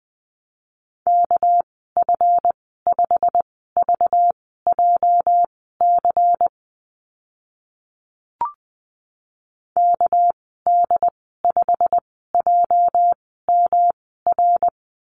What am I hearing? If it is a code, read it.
Morse code, heard as a keyed tone, about 20 wpm, spaced as for 14 wpm, 700 Hz. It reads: KF5VJC KD5JMR
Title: 245 Lesson 39g - Letter z - Call Signs 20wpm